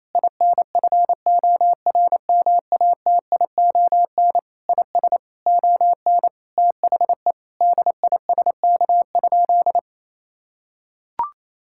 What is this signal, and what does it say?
Morse code, heard as a keyed tone, about 28 wpm, 700 Hz. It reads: INFORMATION IS ON THE DISK?
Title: Sentences from Top 1000 Words - Review Part 6 - 28wpm